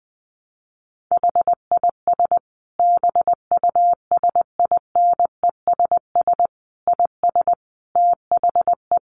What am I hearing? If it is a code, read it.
Morse code, heard as a keyed tone, about 20 wpm, 700 Hz. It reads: HIS BUSINESS IS THE
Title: Sentences from Top 600 Words 20wpm